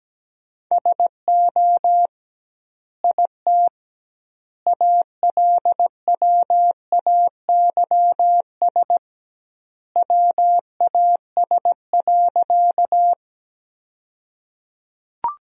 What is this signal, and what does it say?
Morse code, heard as a keyed tone, about 17 wpm, 700 Hz. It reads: SO IT ALWAYS WAS.
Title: 2x Word-Spacing - Sentences from Top 300 Words - 17wpm